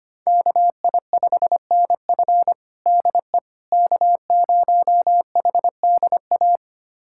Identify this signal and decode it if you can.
Morse code, heard as a keyed tone, about 25 wpm, 700 Hz. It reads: KI5NF DE K0HDA
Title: USA Callsign DE Callsign - 5 Characters - 25wpm